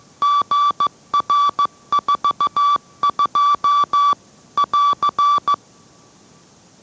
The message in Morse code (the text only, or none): GR42 <AR>